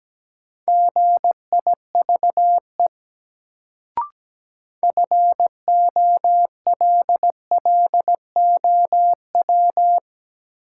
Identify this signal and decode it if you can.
Morse code, heard as a keyed tone, about 17 wpm, 700 Hz. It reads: GIVE FOLLOW
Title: Sentences from Top 200 Words - New Word Drill - 17wpm